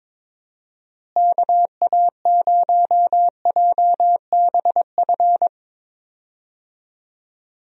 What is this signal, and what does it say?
Morse code, heard as a keyed tone, about 22 wpm, 700 Hz. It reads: KA0JBF
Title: USA Call Signs - 6 Characters - Repeated 3x - 22wpm